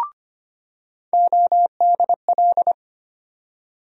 Morse code, heard as a keyed tone, about 25 wpm, 700 Hz. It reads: ODL
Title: USA Call Sign Postfixes - 3 Letters 25wpm